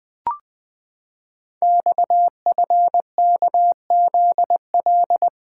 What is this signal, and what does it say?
Morse code, heard as a keyed tone, about 20 wpm, 700 Hz. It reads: XFKZL
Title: Code Groups - 5 Letters - 20wpm